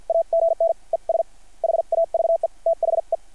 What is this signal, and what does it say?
Morse code, heard as a keyed tone, about 50 wpm, spaced as for 43 wpm, 660 Hz. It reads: AGN ES HAVE THE
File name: archived / CRSnet